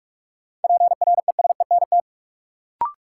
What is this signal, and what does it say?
Morse code, heard as a keyed tone, about 47 wpm, 700 Hz. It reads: PRESENT
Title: Sentences from Top 800 Words - New Word Drill - 45wpm